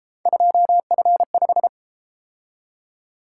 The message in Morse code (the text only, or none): 2F5